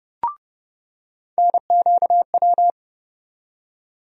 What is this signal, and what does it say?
Morse code, heard as a keyed tone, about 30 wpm, 700 Hz. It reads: NQW
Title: USA Call Sign Postfixes - 3 Letters 30wpm